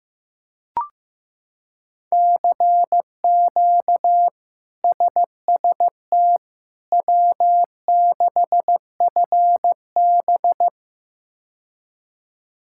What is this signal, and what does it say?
Morse code, heard as a keyed tone, about 16 wpm, 700 Hz. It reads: CQ SST W6FB
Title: K1USN SST QSO Practice - 15wpm